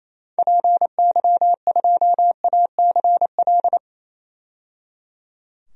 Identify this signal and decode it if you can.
Morse code, heard as a keyed tone, about 28 wpm, 700 Hz. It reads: PY2ACL
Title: DXCC-100 Call Signs - 6 Characters - 28wpm